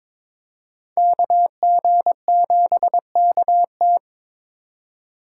Morse code, heard as a keyed tone, about 26 wpm, 700 Hz. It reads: KG7KT
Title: Speed Racing - USA 2x2 Call Signs 22wpm